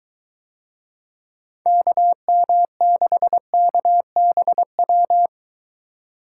Morse code, heard as a keyed tone, about 22 wpm, 700 Hz. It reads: KM6KBW